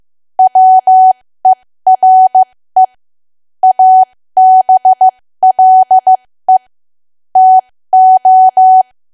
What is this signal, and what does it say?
Morse code, heard as a keyed tone, about 15 wpm, spaced as for 13 wpm, 750 Hz. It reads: WERE ABLE TO
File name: cwa / W1AW-Level2 / 13WPM